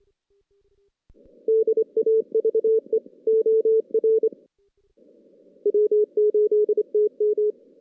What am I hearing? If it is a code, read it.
Morse code, heard as a keyed tone, about 27 wpm, 410 Hz. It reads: DAVE OR W8TM
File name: 2016 JAN CW NAQP / 2016 Jan naqp k5ot - 0-6